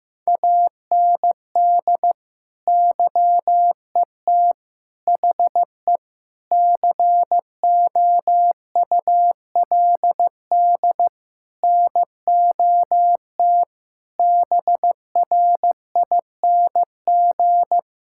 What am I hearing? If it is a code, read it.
Morse code, heard as a keyed tone, about 15 wpm, 700 Hz. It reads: AND YET HE COULD NOT BRING
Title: Sentences from Top 1600 Words 15wpm